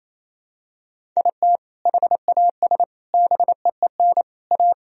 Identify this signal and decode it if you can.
Morse code, heard as a keyed tone, about 28 wpm, 700 Hz. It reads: IT HAS BEEN A
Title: Sentences from Top 1000 Words - Review Part 1 - 28wpm